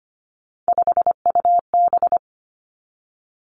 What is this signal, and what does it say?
Morse code, heard as a keyed tone, about 25 wpm, 700 Hz. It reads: 5UB
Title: Three Challenging Letter-Number Combinations 25wpm